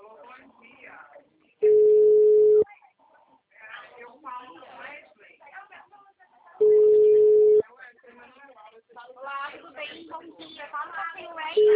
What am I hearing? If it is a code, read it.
Ringback tone (Europe, 425 Hz), with speech.